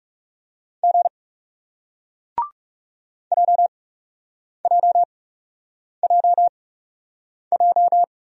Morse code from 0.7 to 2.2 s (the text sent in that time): G